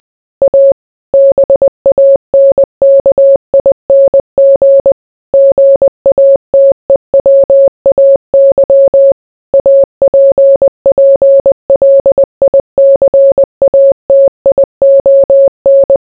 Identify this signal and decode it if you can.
Morse code, heard as a keyed tone, about 20 wpm, 550 Hz. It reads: A BANKING GATEWAY APPLICATION